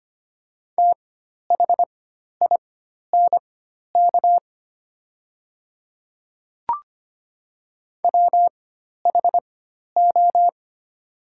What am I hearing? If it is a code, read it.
Morse code, heard as a keyed tone, about 25 wpm, spaced as for 12 wpm, 700 Hz. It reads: THINK WHO
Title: 4x Character Spacing - Top 200 Words - 25wpm